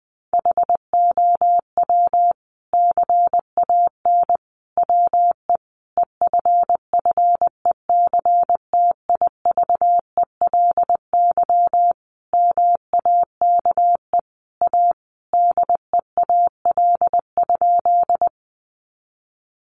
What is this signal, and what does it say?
Morse code, heard as a keyed tone, about 20 wpm, 700 Hz. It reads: HOW CAN WE EFFECTIVELY MAKE A DEAL?
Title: Sentences from Top 1400 Words 20wpm